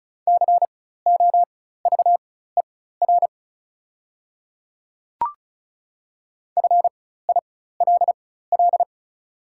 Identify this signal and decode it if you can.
Morse code, heard as a keyed tone, about 36 wpm, spaced as for 16 wpm, 700 Hz. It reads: COVER FILL
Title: 4x Character Spacing - Top 500 Words - 35wpm